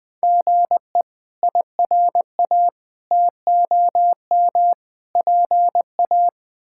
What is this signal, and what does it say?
Morse code, heard as a keyed tone, about 20 wpm, 700 Hz. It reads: GE IRA TOM PA
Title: K1USN SST QSO Practice - 20wpm